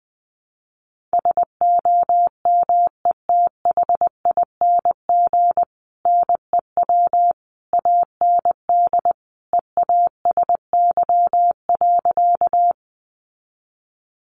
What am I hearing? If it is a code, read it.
Morse code, heard as a keyed tone, about 20 wpm, 700 Hz. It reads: SOMETHING NEW AND EASY.
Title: Sentences from Top 700 Words 20wpm